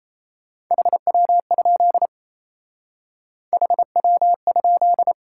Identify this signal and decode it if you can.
Morse code, heard as a keyed tone, about 34 wpm, 700 Hz. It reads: HW? HW?
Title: Speed-Racing Top 180 QSO Elements 28wpm